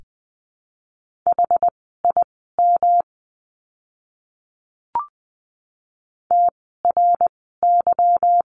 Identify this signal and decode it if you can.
Morse code, heard as a keyed tone, about 20 wpm, spaced as for 15 wpm, 700 Hz. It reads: HIM TRY
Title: 160 Lesson 26f - Letter g - Words 20wpm